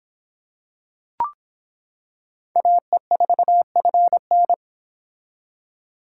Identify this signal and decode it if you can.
Morse code, heard as a keyed tone, about 25 wpm, 700 Hz. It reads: AE4FN